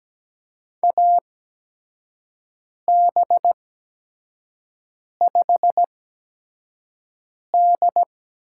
Morse code, heard as a keyed tone, about 17 wpm, spaced as for 5 wpm, 700 Hz. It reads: AB5D
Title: USA Call Signs - 4 Characters - 8x Character Spacing - 17wpm